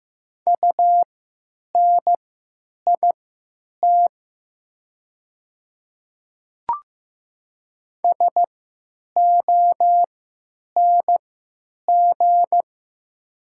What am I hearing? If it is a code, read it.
Morse code, heard as a keyed tone, about 15 wpm, spaced as for 9 wpm, 700 Hz. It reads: UNIT SONG
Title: 3x Character Spacing - Top 500 Words - 15wpm